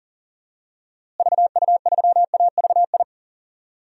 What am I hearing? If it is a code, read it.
Morse code, heard as a keyed tone, about 37 wpm, 700 Hz. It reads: VU3AVI
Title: DXCC-100 Call Signs - 6 Characters - Repeated 2x - 40wpm